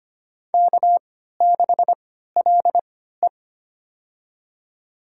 Morse code, heard as a keyed tone, about 25 wpm, spaced as for 14 wpm, 700 Hz. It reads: K6LE
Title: USA Call Signs - 3x Character Spacing - 25wpm